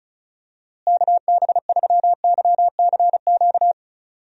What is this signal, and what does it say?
Morse code, heard as a keyed tone, about 34 wpm, 700 Hz. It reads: KB3YCQ
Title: USA Call Signs - 6 Characters - Repeated 3x - 35wpm